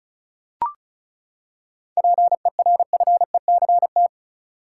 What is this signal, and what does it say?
Morse code, heard as a keyed tone, about 36 wpm, 700 Hz. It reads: PERFECT
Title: Seven Letter Words - 35wpm